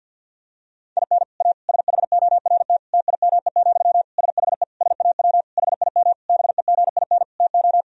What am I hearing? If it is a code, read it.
Morse code, heard as a keyed tone, about 50 wpm, 700 Hz. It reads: IN A SHORT TIME, SHE SAW HIM BEGIN TO
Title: Sentences from Top 1000 Words - Review Part 2 - 50wpm